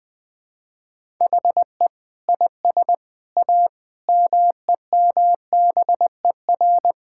Morse code, heard as a keyed tone, about 20 wpm, 700 Hz. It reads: HE IS A MEMBER
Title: Sentences with 3-Word Phrases - 20wpm